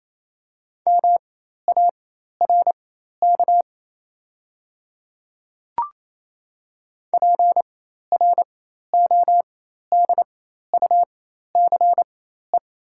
Morse code, heard as a keyed tone, about 27 wpm, spaced as for 13 wpm, 700 Hz. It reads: MARK PRODUCE
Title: 4x Character Spacing - Top 500 Words - 28wpm